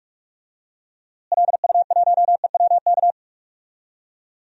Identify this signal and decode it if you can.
Morse code, heard as a keyed tone, about 45 wpm, 700 Hz. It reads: LU1EWK
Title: DXCC-100 Call Signs - 6 Characters - Repeated 3x - 45wpm